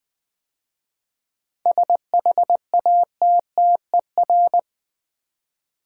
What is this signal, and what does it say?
Morse code, heard as a keyed tone, about 20 wpm, 700 Hz. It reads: SHATTER
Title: 7 Letter Words - First 3 Letters Given - 20wpm